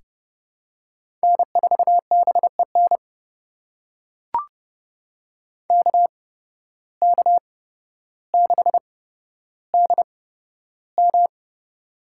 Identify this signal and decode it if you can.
Morse code, heard as a keyed tone, about 30 wpm, spaced as for 13 wpm, 700 Hz. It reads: N4BEN KK6DM
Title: USA Call Signs - 5 Characters - 8x Character Spacing - 30wpm